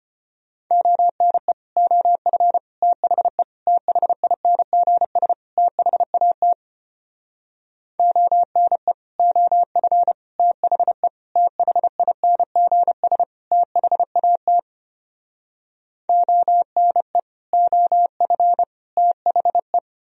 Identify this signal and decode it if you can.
Morse code, heard as a keyed tone, about 33 wpm, 700 Hz. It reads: ONE OF THE THINGS THAT ONE OF THE THINGS THAT ONE OF THE